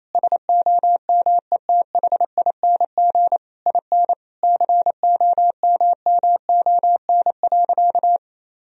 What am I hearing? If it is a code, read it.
Morse code, heard as a keyed tone, about 28 wpm, 700 Hz. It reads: SOMETHING IN COMMON.
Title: Sentences from Top 1000 Words - Review Part 3 - 28wpm